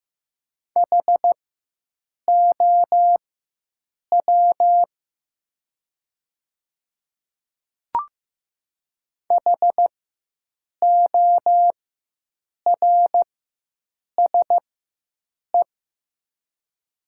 Morse code, heard as a keyed tone, about 15 wpm, spaced as for 7 wpm, 700 Hz. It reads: HOW HORSE
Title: 4x Character Spacing - Top 300 Words - 15wpm